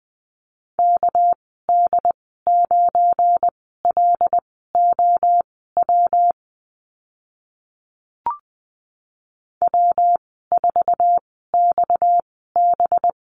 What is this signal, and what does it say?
Morse code, heard as a keyed tone, about 20 wpm, spaced as for 14 wpm, 700 Hz. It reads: KD9LOW W4XB
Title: 231 Lesson 37g - Letter x - Call Signs 20wpm